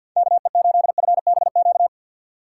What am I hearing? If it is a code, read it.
Morse code, heard as a keyed tone, about 48 wpm, 700 Hz. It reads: KE8FBQ
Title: USA Call Signs - 6 Characters - 4x Character Spacing - 50wpm